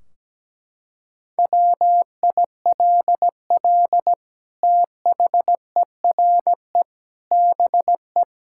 Morse code, heard as a keyed tone, about 17 wpm, 700 Hz. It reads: WILL THERE BE
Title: Sentences from Top 1600 Words 17wpm